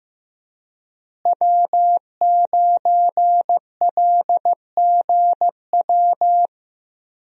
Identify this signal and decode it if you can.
Morse code, heard as a keyed tone, about 15 wpm, 700 Hz. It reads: W9LGW
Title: USA Call Signs - 5 Characters - Repeated 2x - 15wpm